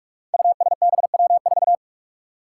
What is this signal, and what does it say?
Morse code, heard as a keyed tone, about 44 wpm, 700 Hz. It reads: USBW4
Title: Five Challenging Letter-Number Combinations 45wpm